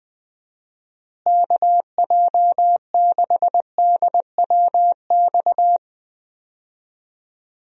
Morse code, heard as a keyed tone, about 21 wpm, 700 Hz. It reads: KJ6DWX
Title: USA Call Signs - 6 Characters - Repeated 3x - 20wpm